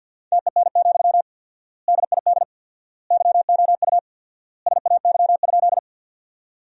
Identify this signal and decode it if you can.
Morse code, heard as a keyed tone, about 53 wpm, 700 Hz. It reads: TEN, DID YOU SAY?
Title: 4x Word-Spacing - Sentences from Top 400 Words - 50wpm